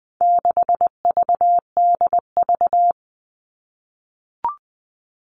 Morse code, heard as a keyed tone, about 20 wpm, 700 Hz. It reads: 6VDV
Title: All Cut Numbers - 4-Digit 20wpm